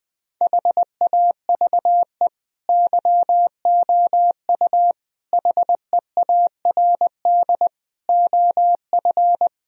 Morse code, heard as a keyed tone, about 21 wpm, 700 Hz. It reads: HAVE YOU HEARD OF
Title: Sentences from Top 1000 Words - Review Part 4 - 20wpm